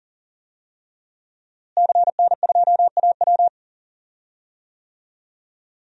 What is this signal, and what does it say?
Morse code, heard as a keyed tone, about 40 wpm, 700 Hz. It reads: CN2AW
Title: DXCC-100 Call Signs - 5 Characters - Repeated 2x - 40wpm